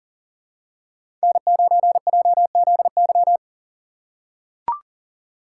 Morse code, heard as a keyed tone, about 38 wpm, 700 Hz. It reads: N9JZY
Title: K1USN SST QSO Practice - 40wpm